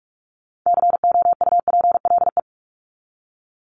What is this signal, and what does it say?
Morse code, heard as a keyed tone, about 44 wpm, 700 Hz. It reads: COUPLE
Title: Top 1001-1250 Words 45wpm